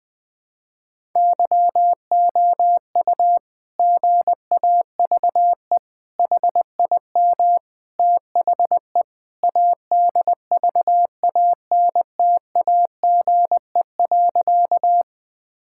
Morse code, heard as a keyed tone, about 20 wpm, 700 Hz. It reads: YOU GAVE HIM THE ADVANTAGE.
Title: Sentences from Top 1000 Words - Review Part 6 - 20wpm